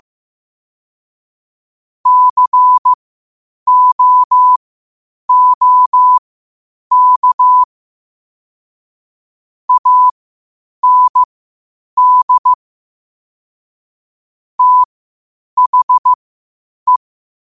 Morse code, heard as a keyed tone, about 15 wpm, spaced as for 8 wpm, 1000 Hz. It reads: COOK AND THE